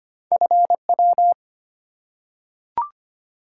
Morse code, heard as a keyed tone, about 25 wpm, 700 Hz. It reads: FW
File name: Two Letter-Number Combinations - 25wpm